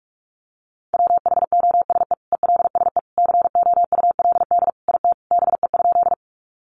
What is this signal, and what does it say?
Morse code, heard as a keyed tone, about 44 wpm, 700 Hz. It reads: WHOSE ELSE COULD IT BE?
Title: Sentences from Top 1200 Words 45wpm